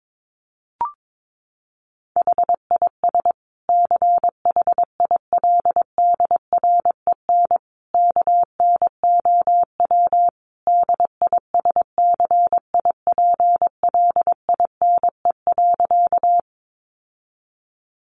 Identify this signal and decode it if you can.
Morse code, heard as a keyed tone, about 22 wpm, 700 Hz. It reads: HIS CHILDREN KNOW DISCIPLINE.
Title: Sentences from Top 900 Words 22wpm